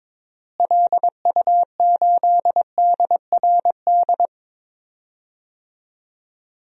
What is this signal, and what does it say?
Morse code, heard as a keyed tone, about 22 wpm, 700 Hz. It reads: LU8DRD
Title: DXCC-100 Call Signs - 6 Characters - Repeated 3x - 22wpm